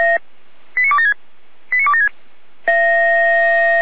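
Five-tone selective call (ZVEI), heard as quick runs of five tones, tones of about 80 ms. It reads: 89276 89276
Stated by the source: ZVEI five-tone selective call received on an SDR.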